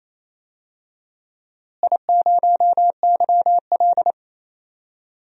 Morse code, heard as a keyed tone, about 28 wpm, 700 Hz. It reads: I0YL
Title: DXCC-100 Call Signs - 4 Characters - Repeated 3x - 28wpm